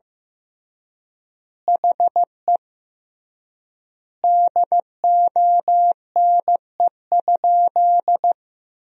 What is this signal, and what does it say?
Morse code, heard as a keyed tone, about 15 wpm, 700 Hz. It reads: HE DONE?
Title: 3x Word-Spacing - Sentences from Top 500 Words - 15wpm